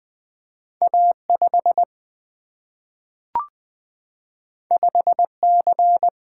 Morse code, heard as a keyed tone, about 20 wpm, 700 Hz. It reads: A5 5C
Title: Single-Letter Single-Number 20wpm